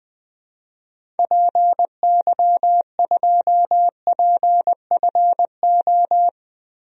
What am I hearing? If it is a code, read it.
Morse code, heard as a keyed tone, about 20 wpm, 700 Hz. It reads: PY2PFO
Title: DXCC-100 Call Signs - 6 Characters - Repeated 2x - 20wpm